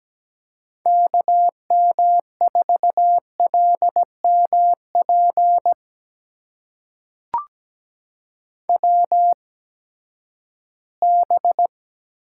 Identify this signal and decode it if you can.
Morse code, heard as a keyed tone, about 17 wpm, spaced as for 12 wpm, 700 Hz. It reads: KM4LMP WB
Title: USA Call Signs - 8x Character Spacing - 17wpm